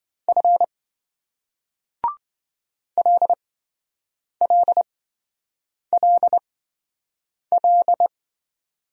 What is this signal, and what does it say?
Morse code, heard as a keyed tone, about 28 wpm, spaced as for 7 wpm, 700 Hz. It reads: F LLLL